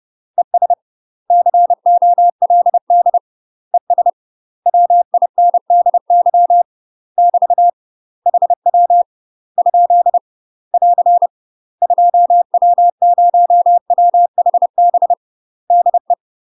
Morse code, heard as a keyed tone, about 30 wpm, 700 Hz. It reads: ES COLD ES WINDY <BT> HW ? <AR> 2W0WHB DE
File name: UK MASTER QSOs TRAINING/30WPM